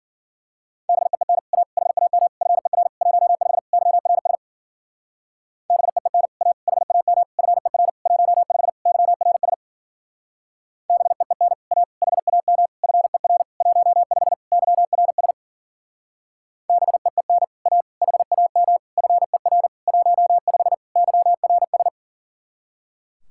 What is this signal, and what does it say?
Morse code, heard as a keyed tone, about 52 wpm, 700 Hz. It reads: BEEN A HAM FER 15 YRS BEEN A HAM FER 15 YRS BEEN A HAM FER 15 YRS BEEN A HAM FER 15 YRS